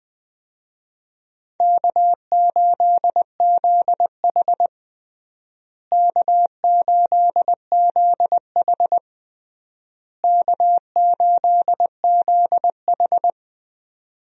Morse code, heard as a keyed tone, about 20 wpm, 700 Hz. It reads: K8ZH K8ZH K8ZH
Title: USA Call Signs - 4 Characters - Repeated 3x - 20wpm